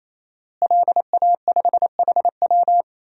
Morse code, heard as a keyed tone, about 28 wpm, 700 Hz. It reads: LA5HW
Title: DXCC-100 Call Signs - 5 Characters - Repeated 3x - 28wpm